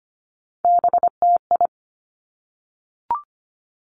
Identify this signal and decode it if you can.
Morse code, heard as a keyed tone, about 25 wpm, 700 Hz. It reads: BTI
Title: Three Challenging Letter-Number Combinations 25wpm